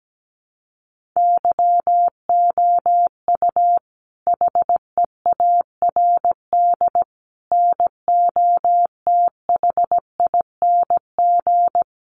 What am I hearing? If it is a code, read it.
Morse code, heard as a keyed tone, about 17 wpm, 700 Hz. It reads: YOU HEARD NOTHING
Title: Sentences from Top 600 Words 17wpm